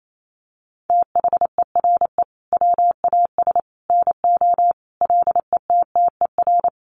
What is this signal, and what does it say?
Morse code, heard as a keyed tone, about 28 wpm, 700 Hz. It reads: THERE WAS NO LETTER
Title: Sentences from Top 300 Words 28wpm